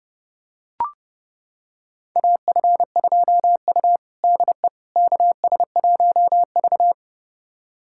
Morse code, heard as a keyed tone, about 30 wpm, 700 Hz. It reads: AF2U DE KS1V
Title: USA Callsign DE Callsign - 4 Characters - 30wpm